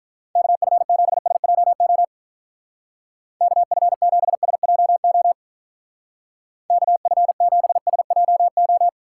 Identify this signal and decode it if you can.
Morse code, heard as a keyed tone, about 44 wpm, 700 Hz. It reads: KF7SJO KF7SJO KF7SJO